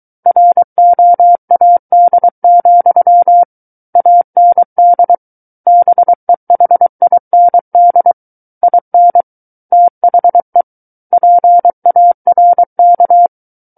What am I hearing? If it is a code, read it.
Morse code, heard as a keyed tone, about 23 wpm, 700 Hz. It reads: ROAD, AND BEHIND IN THE PARK
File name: Book One / 14-Chapter 14 - 23 WPM